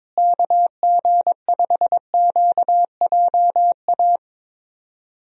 Morse code, heard as a keyed tone, about 22 wpm, 700 Hz. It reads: KG5QJA
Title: USA Call Signs - 6 Characters - Repeated 3x - 22wpm